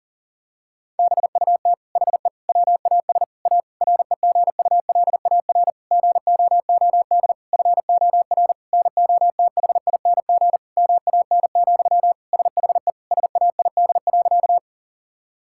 Morse code, heard as a keyed tone, about 40 wpm, 700 Hz. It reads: BUT HE WAS A REGULAR GOOD FOR NOTHING MAN, SHE SAID.
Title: Sentences from Top 1000 Words - Review Part 6 - 40wpm